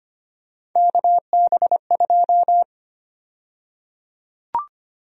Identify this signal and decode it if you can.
Morse code, heard as a keyed tone, about 25 wpm, 700 Hz. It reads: KB2
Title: USA Call Sign Prefixes - 25wpm